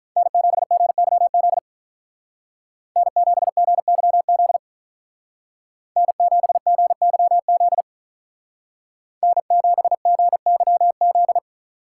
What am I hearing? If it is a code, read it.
Morse code, heard as a keyed tone, about 46 wpm, 700 Hz. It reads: N7GYZ N7GYZ N7GYZ N7GYZ